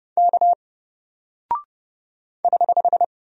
Morse code, heard as a keyed tone, about 30 wpm, 700 Hz. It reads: K <HH>
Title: Single Character (Letter:Number:Punctuation:Prosign) Alt 30wpm